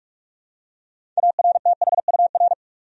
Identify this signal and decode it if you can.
Morse code, heard as a keyed tone, about 46 wpm, 700 Hz. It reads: ARTHUR
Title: USA Common Names - 45wpm